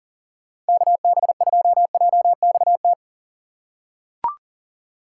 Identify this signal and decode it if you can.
Morse code, heard as a keyed tone, about 37 wpm, 700 Hz. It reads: KB2JXT
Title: USA Call Signs - 2x Character Spacing - 40wpm